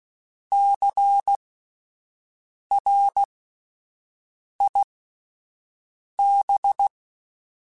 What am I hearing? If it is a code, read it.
Morse code, heard as a keyed tone, about 16 wpm, spaced as for 5 wpm, 800 Hz. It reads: CRIB
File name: Alphabetical / Alphabetical mp3s / Cs @5